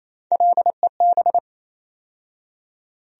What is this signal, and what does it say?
Morse code, heard as a keyed tone, about 27 wpm, 700 Hz. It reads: LEB
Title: USA Call Sign Postfixes - 3 Letters 28wpm